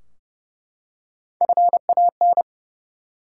Morse code, heard as a keyed tone, about 30 wpm, 700 Hz. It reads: FAN
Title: Three Letter Words - 30wpm